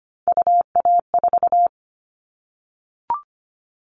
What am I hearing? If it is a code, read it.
Morse code, heard as a keyed tone, about 25 wpm, 700 Hz. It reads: UA4
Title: All Cut Numbers - 3-Digit 25wpm